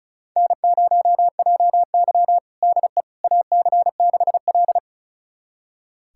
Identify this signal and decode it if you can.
Morse code, heard as a keyed tone, about 36 wpm, 700 Hz. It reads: N0JY DE AC6L
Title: USA Callsign DE Callsign - 4 Characters - 35wpm